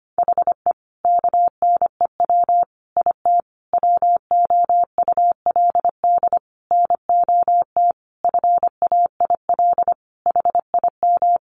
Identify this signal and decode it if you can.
Morse code, heard as a keyed tone, about 25 wpm, 700 Hz. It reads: HE KNEW IT WOULD NOT FAIL HIM
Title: Sentences from Top 1100 Words 25wpm